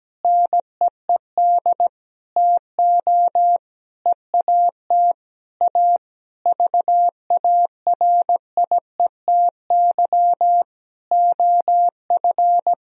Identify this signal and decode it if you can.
Morse code, heard as a keyed tone, about 17 wpm, 700 Hz. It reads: NEED TO EAT A VARIETY OF
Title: Sentences from Top 1000 Words - Review Part 4 - 17wpm